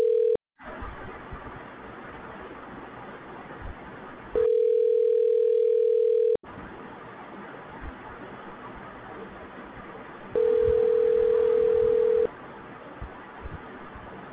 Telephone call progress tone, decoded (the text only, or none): ringback tone